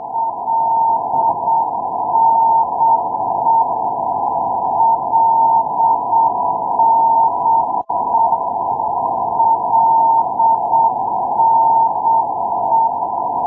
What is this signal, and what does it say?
Morse code, heard as a keyed tone, about 7 wpm, 910 Hz, with noise. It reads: LNE LNE LNE
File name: ndb milano-linate lne-386 lago d idro